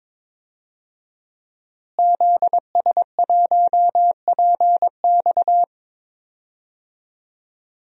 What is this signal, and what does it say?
Morse code, heard as a keyed tone, about 22 wpm, 700 Hz. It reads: ZS1PX
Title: DXCC-100 Call Signs - 5 Characters - Repeated 2x - 22wpm